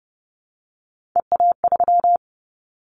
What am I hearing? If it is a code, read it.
Morse code, heard as a keyed tone, about 30 wpm, 700 Hz. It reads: EA3
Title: Callsign Prefixes for Top 25 DXCC - 30wpm